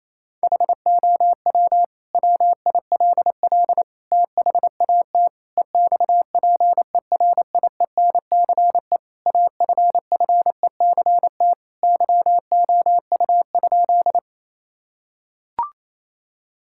Morse code, heard as a keyed tone, about 28 wpm, 700 Hz. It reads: HOW WILL THAT EXPERIENCE AFFECT YOU?
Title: Sentences from Top 1000 Words - Review Part 6 - 28wpm